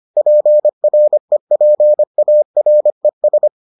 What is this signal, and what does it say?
Morse code, heard as a keyed tone, about 25 wpm, 600 Hz. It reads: PREPARES